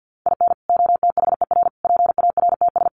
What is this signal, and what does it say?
Morse code, heard as a keyed tone, about 50 wpm, 700 Hz. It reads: IN OTHER PARTS
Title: Sets of 5 Words 50wpm